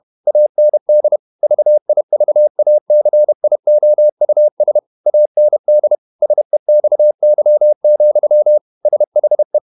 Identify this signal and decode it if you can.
Morse code, heard as a keyed tone, about 31 wpm, 600 Hz. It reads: AND VIVACIOUS AND SEXY, SHE
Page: Never 31wpm | CW med Gnister